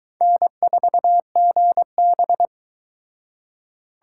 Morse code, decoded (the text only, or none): N4GB